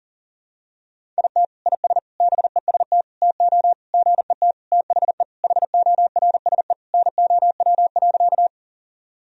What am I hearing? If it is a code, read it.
Morse code, heard as a keyed tone, about 40 wpm, 700 Hz. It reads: IT IS BEST TO GET THE HORSE NOW.
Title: Sentences from Top 1000 Words - Review Part 1 - 40wpm